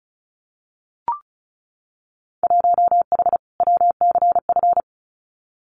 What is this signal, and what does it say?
Morse code, heard as a keyed tone, about 36 wpm, 700 Hz. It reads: 1H WCF
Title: Winter Field Day Exchanges Encore 35wpm